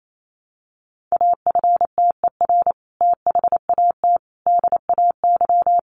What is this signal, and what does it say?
Morse code, heard as a keyed tone, about 28 wpm, 700 Hz. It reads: AFTER THAT DAY
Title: Sentences from Top 1100 Words 28wpm